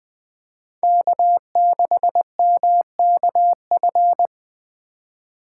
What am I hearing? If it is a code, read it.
Morse code, heard as a keyed tone, about 20 wpm, 700 Hz. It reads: K6MKF
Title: K1USN SST QSO Practice - 20wpm